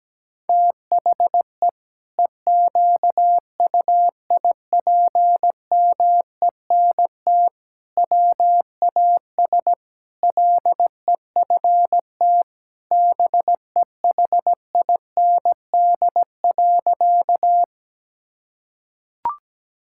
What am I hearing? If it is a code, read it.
Morse code, heard as a keyed tone, about 17 wpm, 700 Hz. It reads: THE EQUIPMENT WAS LEFT BEHIND.
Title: Sentences from Top 1000 Words - Review Part 5 - 17wpm